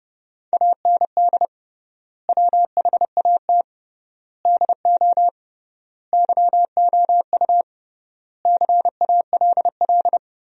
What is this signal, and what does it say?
Morse code, heard as a keyed tone, about 30 wpm, 700 Hz. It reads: AND WHAT DO YOU CALL
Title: 3x Word-Spacing - Sentences from Top 100 Words - 30wpm